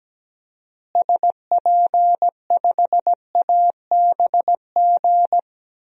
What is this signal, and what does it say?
Morse code, heard as a keyed tone, about 17 wpm, 700 Hz. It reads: SP5ABG
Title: DXCC-100 Call Signs - 6 Characters - Repeated 2x - 17wpm